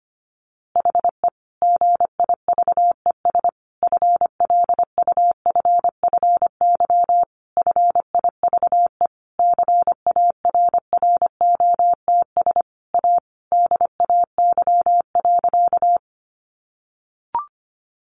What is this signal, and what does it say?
Morse code, heard as a keyed tone, about 25 wpm, 700 Hz. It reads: HE GIVES FLUFFY FIVE CARROTS A DAY.
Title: Easy Phrases 25wpm